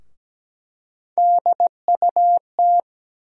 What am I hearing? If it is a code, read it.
Morse code, heard as a keyed tone, about 17 wpm, 700 Hz. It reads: DUT
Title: USA Call Sign Postfixes - 3 Letters 17wpm